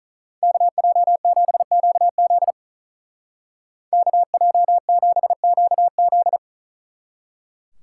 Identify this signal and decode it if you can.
Morse code, heard as a keyed tone, about 44 wpm, 700 Hz. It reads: KJ7QZ KJ7QZ